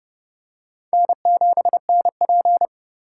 Morse code, heard as a keyed tone, about 30 wpm, 700 Hz. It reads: N7NP
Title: USA Call Signs - 4 Characters - 8x Character Spacing - 30wpm